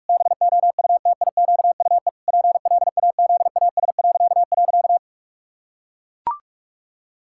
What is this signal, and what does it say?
Morse code, heard as a keyed tone, about 44 wpm, 700 Hz. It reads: BOUTIQUE PLAZAS..
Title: Pangram Challenge v2 - 45wpm